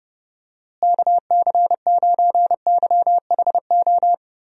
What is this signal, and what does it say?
Morse code, heard as a keyed tone, about 30 wpm, 700 Hz. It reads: KC9YHO
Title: USA Call Signs - 6 Characters - Repeated 2x - 30wpm